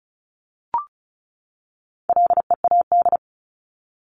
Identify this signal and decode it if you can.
Morse code, heard as a keyed tone, about 34 wpm, 700 Hz. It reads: LEAD
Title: Top 500 Words 35wpm